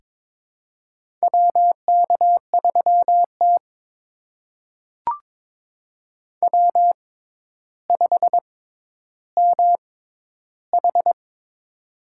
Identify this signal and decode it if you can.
Morse code, heard as a keyed tone, about 22 wpm, spaced as for 11 wpm, 700 Hz. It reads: WK3T W5MH
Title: USA Call Signs - 4 Characters - 6x Character Spacing - 22wpm